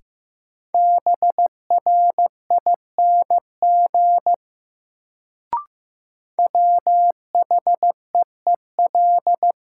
Morse code, heard as a keyed tone, about 15 wpm, 700 Hz. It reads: BRING WHEEL
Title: Sentences from Top 500 Words - New Word Drill - 15wpm